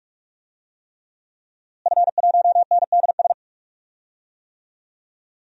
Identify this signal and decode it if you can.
Morse code, heard as a keyed tone, about 46 wpm, 700 Hz. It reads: F1NDS
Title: DXCC-100 Call Signs - 5 Characters - Repeated 3x - 45wpm